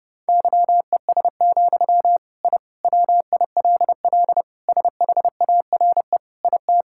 Morse code, heard as a keyed tone, about 29 wpm, 700 Hz. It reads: YES, I WILL SHARE IT
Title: Sentences from Top 1000 Words - Review Part 6 - 30wpm